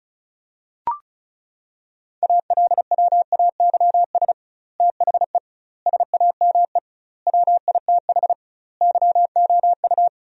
Morse code, heard as a keyed tone, about 36 wpm, 700 Hz. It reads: ALWAYS THE SAME WITH YOU
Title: 2x Word-Spacing - Sentences from Top 300 Words - 35wpm